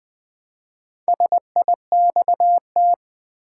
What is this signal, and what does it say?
Morse code, heard as a keyed tone, about 20 wpm, 700 Hz. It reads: SIXT
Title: Build A Word - 1 Letter At A Time - 3 to 5-Letter Words - 20wpm